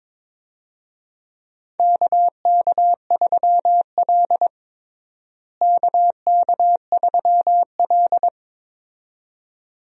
Morse code, heard as a keyed tone, about 22 wpm, 700 Hz. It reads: KK3L KK3L
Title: USA Call Signs - 4 Characters - Repeated 2x - 22wpm